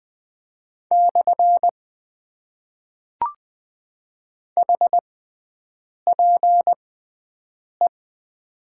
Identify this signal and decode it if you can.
Morse code, heard as a keyed tone, about 20 wpm, spaced as for 7 wpm, 700 Hz. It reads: / HPE
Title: Top 90 QSO Elements - 6x Character Spacing - 20wpm